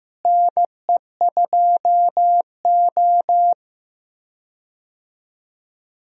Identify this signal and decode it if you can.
Morse code, heard as a keyed tone, about 15 wpm, 700 Hz. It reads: NE2O
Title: USA Call Signs - 4 Characters - Repeated 3x - 15wpm